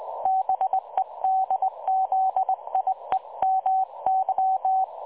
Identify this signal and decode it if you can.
Morse code, heard as a keyed tone, about 20 wpm, 760 Hz. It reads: BEDZIEMY
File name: dlazaawansowanych / nagraniazpasma / websdr recording 3565.0kHz